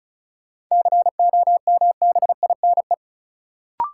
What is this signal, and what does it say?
Morse code, heard as a keyed tone, about 36 wpm, 700 Hz. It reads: COMBINE
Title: Sentences from Top 1600 Words - New Word Drill - 35wpm